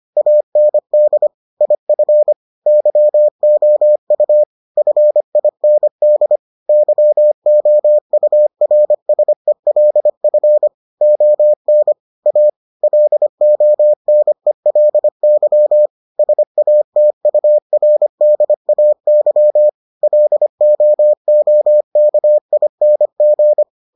Morse code, heard as a keyed tone, about 25 wpm, 600 Hz. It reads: AND IF YOU FIND YOURSELF ON A LONELY SATURDAY LOOKING